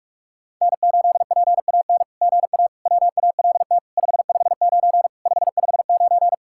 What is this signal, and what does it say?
Morse code, heard as a keyed tone, about 44 wpm, 690 Hz. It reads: N8PAN GA WALT 559 559
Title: SOTA QSO Practice 45wpm